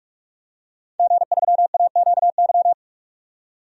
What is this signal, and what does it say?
Morse code, heard as a keyed tone, about 43 wpm, 700 Hz. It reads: G3AQY
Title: DXCC-100 Call Signs - 5 Characters - 45wpm